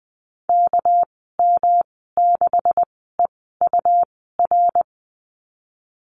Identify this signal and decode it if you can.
Morse code, heard as a keyed tone, about 20 wpm, spaced as for 15 wpm, 700 Hz. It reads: KM6EUR
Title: 200 Lesson 32g - Letter k - Call Signs 20wpm